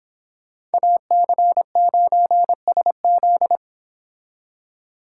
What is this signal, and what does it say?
Morse code, heard as a keyed tone, about 30 wpm, 700 Hz. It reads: AC9SZ